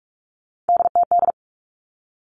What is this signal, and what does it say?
Morse code, heard as a keyed tone, about 46 wpm, 700 Hz. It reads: DTD
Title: Three Challenging Letter-Number Combinations 45wpm